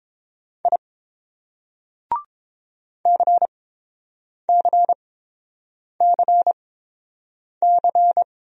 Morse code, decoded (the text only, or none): I CCCC